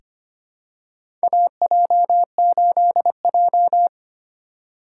Morse code, heard as a keyed tone, about 25 wpm, 700 Hz. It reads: AJ8J
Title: USA Call Signs - 4 Characters - 3x Character Spacing - 25wpm